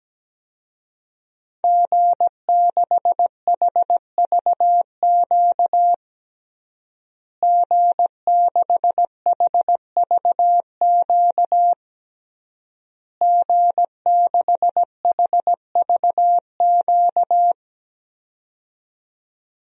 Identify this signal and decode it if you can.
Morse code, heard as a keyed tone, about 17 wpm, 700 Hz. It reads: G6HVQ G6HVQ G6HVQ
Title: DXCC-100 Call Signs - 5 Characters - Repeated 3x - 17wpm